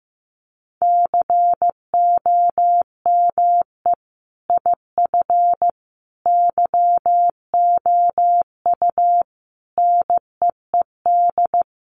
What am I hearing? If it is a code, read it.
Morse code, heard as a keyed tone, about 15 wpm, 700 Hz. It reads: COME IF YOU NEED
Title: Sentences from Top 1000 Words 15wpm